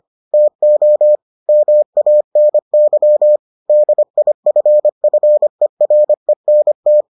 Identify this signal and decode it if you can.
Morse code, heard as a keyed tone, about 25 wpm, 600 Hz. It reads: TO MANY DIFFERENT